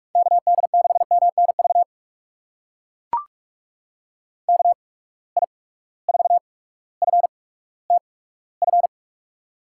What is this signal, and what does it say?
Morse code, heard as a keyed tone, about 43 wpm, spaced as for 19 wpm, 700 Hz. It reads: KD6MNV KI4FTF
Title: USA Call Signs - 6 Characters - 8x Character Spacing - 45wpm